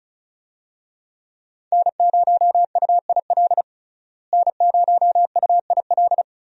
Morse code, heard as a keyed tone, about 36 wpm, 700 Hz. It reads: N0UIL N0UIL
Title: USA Call Signs - 5 Characters - Repeated 2x - 35wpm